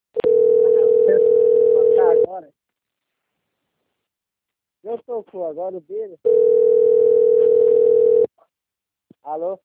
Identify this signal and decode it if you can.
Ringback tone (North America), 440 + 480 Hz, with speech.
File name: rg-1600-unknown- - - .64647